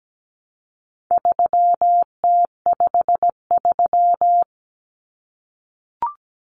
Morse code, heard as a keyed tone, about 17 wpm, 700 Hz. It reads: 3T53
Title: Common Cut Numbers - 4-Digit 17wpm